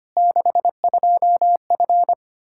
Morse code, heard as a keyed tone, about 25 wpm, 700 Hz. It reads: 62F
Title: K1USN SST QSO Practice 25wpm